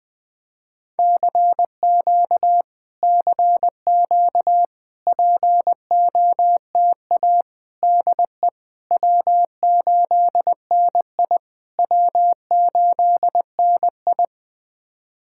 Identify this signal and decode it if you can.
Morse code, heard as a keyed tone, about 20 wpm, 690 Hz. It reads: CQ CQ POTA DE W8NI W8NI
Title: POTA QSO Practice - 20wpm